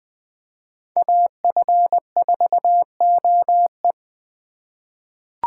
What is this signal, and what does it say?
Morse code, heard as a keyed tone, about 20 wpm, 700 Hz. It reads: AF4OE
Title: USA Call Signs - 4x Character Spacing - 20wpm